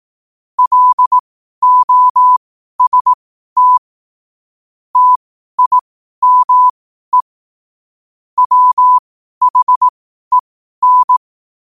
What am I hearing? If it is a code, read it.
Morse code, heard as a keyed tone, about 18 wpm, spaced as for 12 wpm, 1000 Hz. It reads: LOST TIME WHEN
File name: qotdfiles / 12wpm